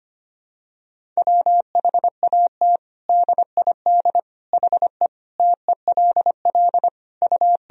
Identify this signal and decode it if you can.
Morse code, heard as a keyed tone, about 25 wpm, 700 Hz. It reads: WHAT DID HE TELL U
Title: Sentences from Top 200 Words and Abbreviations 25wpm